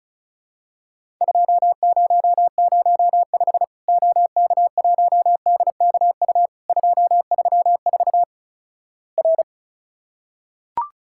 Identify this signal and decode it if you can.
Morse code, heard as a keyed tone, about 36 wpm, 700 Hz. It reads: 2005 OK1DKU 234 R
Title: Worked All Europe DX Contest - QTC Exchange - 35wpm